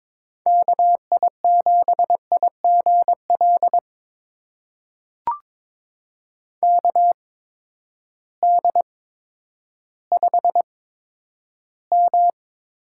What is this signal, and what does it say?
Morse code, heard as a keyed tone, about 22 wpm, spaced as for 11 wpm, 700 Hz. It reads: KI7IGL KD5M
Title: USA Call Signs - 6 Characters - 8x Character Spacing - 22wpm